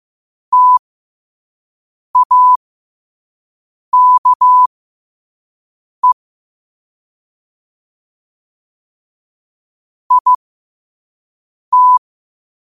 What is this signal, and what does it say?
Morse code, heard as a keyed tone, about 15 wpm, spaced as for 5 wpm, 1000 Hz. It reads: TAKE IT
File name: qotdfiles / 5wpm